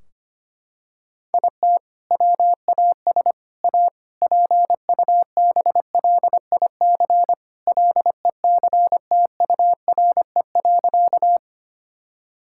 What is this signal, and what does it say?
Morse code, heard as a keyed tone, about 25 wpm, 700 Hz. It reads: IT WAS A PUBLIC LECTURE.
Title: Sentences from Top 1600 Words 25wpm